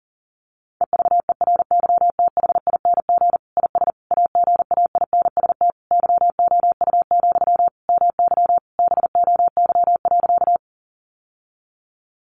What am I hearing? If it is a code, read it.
Morse code, heard as a keyed tone, about 40 wpm, 700 Hz. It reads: EVERYTHING IS AGAINST YOU, MY BOY.
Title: Sentences from Top 700 Words 40wpm